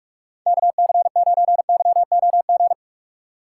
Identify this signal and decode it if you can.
Morse code, heard as a keyed tone, about 44 wpm, 700 Hz. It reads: KC9YOG
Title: USA Call Signs - 2x Character Spacing - 45wpm